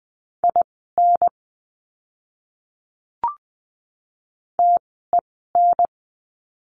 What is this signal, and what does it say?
Morse code, heard as a keyed tone, about 20 wpm, spaced as for 14 wpm, 700 Hz. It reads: IN TEN
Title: 030 Lesson 6f - Letter i - Words 20wpm